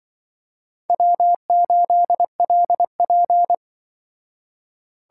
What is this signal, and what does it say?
Morse code, heard as a keyed tone, about 25 wpm, 700 Hz. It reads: W8LP
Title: Speed Racing - USA 1x2 Call Signs 20wpm